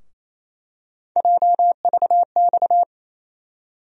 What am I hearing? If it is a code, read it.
Morse code, heard as a keyed tone, about 28 wpm, 700 Hz. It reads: JVX
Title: USA Call Sign Postfixes - 3 Letters 28wpm